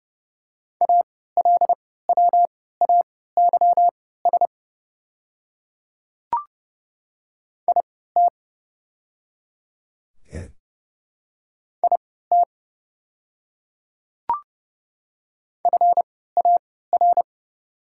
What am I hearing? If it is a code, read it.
Morse code, heard as a keyed tone, about 30 wpm, spaced as for 17 wpm, 700 Hz. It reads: ALWAYS IT IT FAR
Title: 3x Character Spacing - Top 300 Words - 30wpm